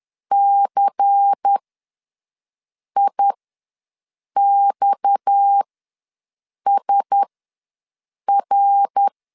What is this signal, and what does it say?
Morse code, heard as a keyed tone, about 10 wpm, spaced as for 6 wpm, 800 Hz. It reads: CIXSR
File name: LEZIONI / 8 YJ45 / 8 YJ ws 10wpm 4043